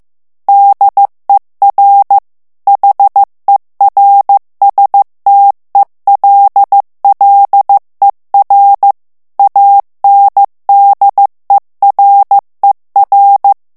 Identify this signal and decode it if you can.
Morse code, heard as a keyed tone, about 15 wpm, 800 Hz. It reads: DER HERSTELLER ANDERER
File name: morse mp3 / ALLEMAND texte clair 1 vitesse 14 mots minute